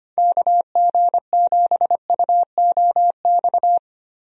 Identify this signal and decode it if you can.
Morse code, heard as a keyed tone, about 25 wpm, 700 Hz. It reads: KG7UOX
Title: USA Call Signs - 6 Characters - Repeated 2x - 25wpm